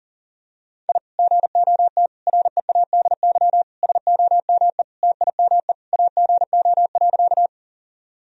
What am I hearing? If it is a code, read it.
Morse code, heard as a keyed tone, about 40 wpm, 700 Hz. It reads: I GOT READY SOME TIME AGO.
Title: Sentences from Top 1000 Words - Review Part 3 - 40wpm